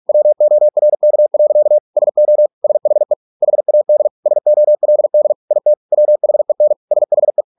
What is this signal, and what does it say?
Morse code, heard as a keyed tone, about 46 wpm, 600 Hz. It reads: WORK. SO SHE HAD SOLD IT WHEN SHE